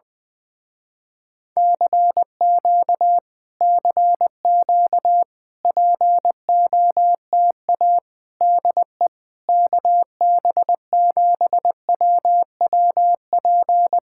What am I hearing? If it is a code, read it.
Morse code, heard as a keyed tone, about 20 wpm, 700 Hz. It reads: CQ CQ POTA DE KB7WWP
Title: Speed-Racing Common QSO Phrases - 17wpm